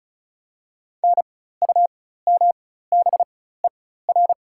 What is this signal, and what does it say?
Morse code, heard as a keyed tone, about 36 wpm, spaced as for 16 wpm, 700 Hz. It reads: NUMBER
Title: 4x Character Spacing - Top 200 Words - 35wpm